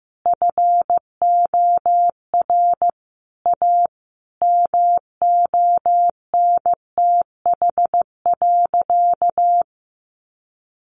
Morse code, heard as a keyed tone, about 15 wpm, 700 Hz. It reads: FOR A MONTH.
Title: Easy Phrases 15wpm